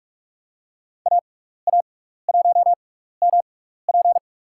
Morse code, heard as a keyed tone, about 46 wpm, spaced as for 16 wpm, 700 Hz. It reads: AA1MP
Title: USA Call Signs - 6x Character Spacing - 45wpm